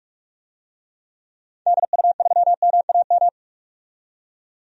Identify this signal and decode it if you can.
Morse code, heard as a keyed tone, about 42 wpm, 700 Hz. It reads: DU3MAM
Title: DXCC-100 Call Signs - 6 Characters - 45wpm